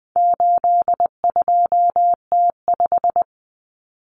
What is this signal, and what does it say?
Morse code, heard as a keyed tone, about 20 wpm, 700 Hz. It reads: 82T5
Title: Common Cut Numbers - 4-Digit 20wpm